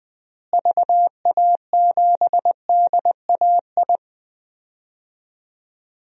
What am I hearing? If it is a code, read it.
Morse code, heard as a keyed tone, about 20 wpm, 700 Hz. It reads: VA7DAI
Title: DXCC-100 Call Signs - 6 Characters - Repeated 3x - 20wpm